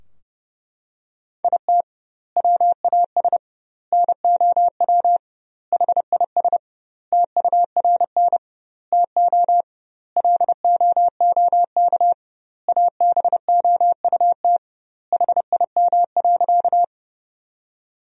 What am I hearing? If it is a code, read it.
Morse code, heard as a keyed tone, about 32 wpm, 700 Hz. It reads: IT WAS NOW HIS TURN TO LOOK ABOUT HIM.
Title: 2x Word-Spacing - Sentences from Top 200 Words - 30wpm